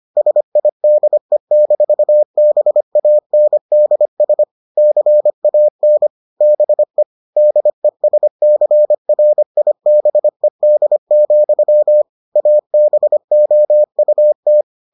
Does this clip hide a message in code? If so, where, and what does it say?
Yes: Morse code from 0.2 to 14.6 s, heard as a keyed tone, about 25 wpm, 600 Hz. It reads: SIDE-BANDS CAN BE DESCRIBED, ABOUT